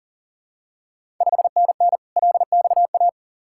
Morse code, heard as a keyed tone, about 40 wpm, 690 Hz. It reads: 5NN LXA
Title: Ontario QSO Party Exchange 40wpm